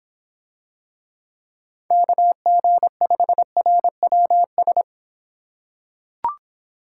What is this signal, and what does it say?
Morse code, heard as a keyed tone, about 26 wpm, 700 Hz. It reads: KG5RWS